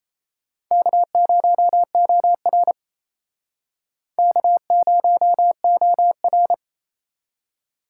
Morse code, heard as a keyed tone, about 34 wpm, 700 Hz. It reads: K0OR K0OR
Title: Speed Racing - USA 1x2 Call Signs 28wpm